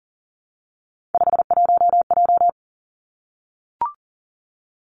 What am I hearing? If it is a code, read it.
Morse code, heard as a keyed tone, about 42 wpm, 700 Hz. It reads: 51J
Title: Three Challenging Letter-Number Combinations 40wpm